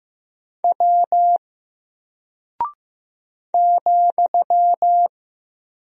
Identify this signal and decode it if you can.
Morse code, heard as a keyed tone, about 15 wpm, 700 Hz. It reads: W ,
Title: Single Character (Letter:Number:Punctuation:Prosign) Alt 15wpm